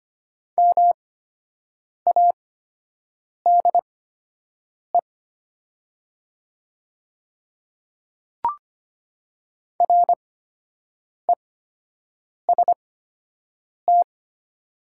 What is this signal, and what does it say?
Morse code, heard as a keyed tone, about 25 wpm, spaced as for 7 wpm, 700 Hz. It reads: MADE REST
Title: 8x Character Spacing - Top 500 Words - 25wpm